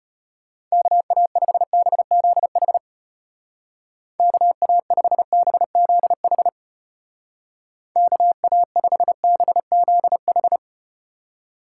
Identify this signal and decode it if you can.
Morse code, heard as a keyed tone, about 32 wpm, 700 Hz. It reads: KA5BZH KA5BZH KA5BZH